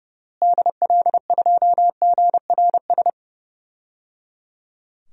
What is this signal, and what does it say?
Morse code, heard as a keyed tone, about 30 wpm, 700 Hz. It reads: DL2GRS
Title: DXCC-100 Call Signs - 6 Characters - Repeated 3x - 30wpm